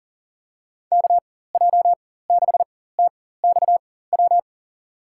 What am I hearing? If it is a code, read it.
Morse code, heard as a keyed tone, about 40 wpm, spaced as for 19 wpm, 700 Hz. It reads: KJ6TXW
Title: USA Call Signs - 6 Characters - 4x Character Spacing - 40wpm